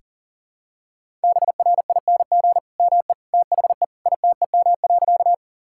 Morse code, heard as a keyed tone, about 37 wpm, 700 Hz. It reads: BRING ME THE ITEM.
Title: Sentences from Top 1000 Words - Review Part 6 - 40wpm